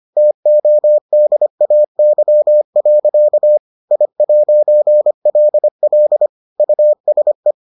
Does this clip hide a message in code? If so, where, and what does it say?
Yes: Morse code from 0.2 to 7.5 s, heard as a keyed tone, about 25 wpm, 600 Hz. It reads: TODAY. I'LL USE